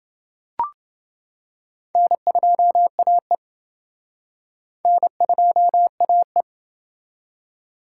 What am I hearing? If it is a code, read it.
Morse code, heard as a keyed tone, about 28 wpm, 700 Hz. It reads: N2AE N2AE